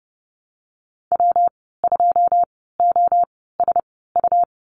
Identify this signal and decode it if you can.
Morse code, heard as a keyed tone, about 30 wpm, spaced as for 17 wpm, 700 Hz. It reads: W2OSU
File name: Lesson 19g - Letter w - Call Signs 30wpm